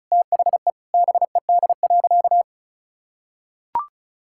Morse code, heard as a keyed tone, about 36 wpm, 700 Hz. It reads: THE BED.
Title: Sentences from Top 1000 Words - Review Part 6 - 35wpm